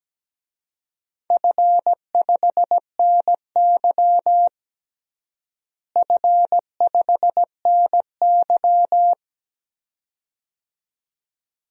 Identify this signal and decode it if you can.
Morse code, heard as a keyed tone, about 17 wpm, 700 Hz. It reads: F5NY F5NY
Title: DXCC-100 Call Signs - 4 Characters - Repeated 2x - 17wpm